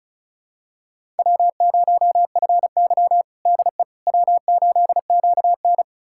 Morse code, heard as a keyed tone, about 34 wpm, 700 Hz. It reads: W0FY DE W8QN
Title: USA Callsign DE Callsign - 4 Characters - 35wpm